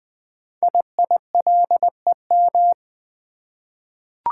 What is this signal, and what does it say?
Morse code, heard as a keyed tone, about 20 wpm, 700 Hz. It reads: IILEM
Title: Code Groups - 5 Letter:Numbers - 20wpm